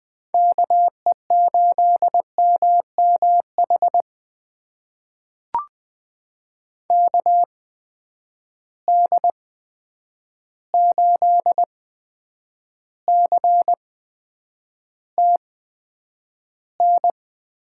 Morse code, heard as a keyed tone, about 20 wpm, spaced as for 9 wpm, 700 Hz. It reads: KE8MMH KD8CTN
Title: USA Call Signs - 6 Characters - 8x Character Spacing - 20wpm